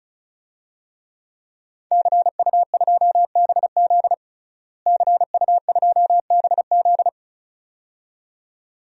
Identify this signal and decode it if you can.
Morse code, heard as a keyed tone, about 36 wpm, 700 Hz. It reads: CU2BZ CU2BZ
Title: DXCC-100 Call Signs - 5 Characters - Repeated 2x - 35wpm